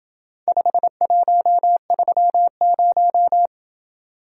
Morse code, heard as a keyed tone, about 27 wpm, 700 Hz. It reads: 5130